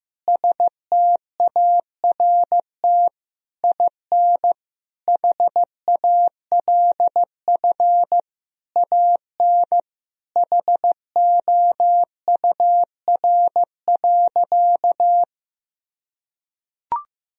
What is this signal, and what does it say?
Morse code, heard as a keyed tone, about 15 wpm, 700 Hz. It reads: START IN HALF AN HOUR.
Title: Sentences from Top 1000 Words - Review Part 1 - 15wpm